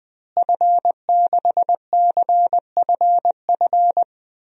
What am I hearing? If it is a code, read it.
Morse code, heard as a keyed tone, about 20 wpm, 700 Hz. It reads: F6CFF
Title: DXCC-100 Call Signs - 5 Characters - Repeated 3x - 20wpm